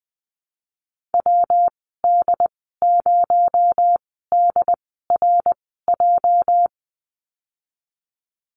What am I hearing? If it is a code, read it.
Morse code, heard as a keyed tone, about 20 wpm, spaced as for 15 wpm, 700 Hz. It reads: WD0DRJ
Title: 231 Lesson 37g - Letter x - Call Signs 20wpm